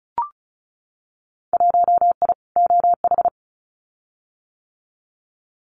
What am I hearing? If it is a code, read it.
Morse code, heard as a keyed tone, about 34 wpm, 700 Hz. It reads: 1I OH
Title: Winter Field Day Exchanges 35wpm